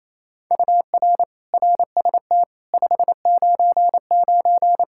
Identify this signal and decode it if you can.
Morse code, heard as a keyed tone, about 28 wpm, 700 Hz. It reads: UR RST 599
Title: Practice QSO - Exchange 1 - Beginning - 28wpm